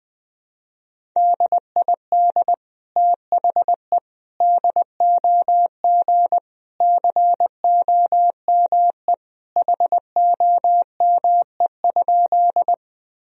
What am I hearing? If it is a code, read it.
Morse code, heard as a keyed tone, about 20 wpm, 700 Hz. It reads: DID THE DOG COME HOME?
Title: Sentences from Top 1000 Words - Review Part 2 - 20wpm